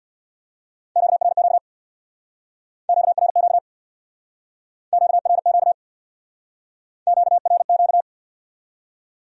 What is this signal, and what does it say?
Morse code, heard as a keyed tone, about 62 wpm, 700 Hz. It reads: QRQ QRQ QRQ QRQ
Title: Speed-Racing Q-Codes - 50wpm